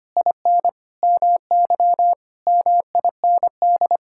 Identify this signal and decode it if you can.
Morse code, heard as a keyed tone, about 25 wpm, 700 Hz. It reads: IN MY MIND
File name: practice / 25wpm / Sentences with 9-Word Phrases - 25wpm